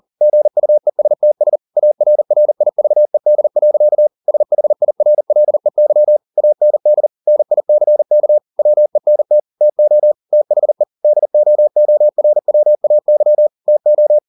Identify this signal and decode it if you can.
Morse code, heard as a keyed tone, about 40 wpm, 600 Hz. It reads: GUESTS ARRIVED. SHIRLEY AND NICK WENT TO THE DOORWAY TO